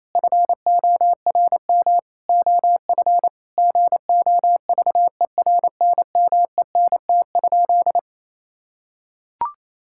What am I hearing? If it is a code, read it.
Morse code, heard as a keyed tone, about 28 wpm, 700 Hz. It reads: FORM OF GOVERNMENT?
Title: Sentences from Top 1000 Words - Review Part 6 - 28wpm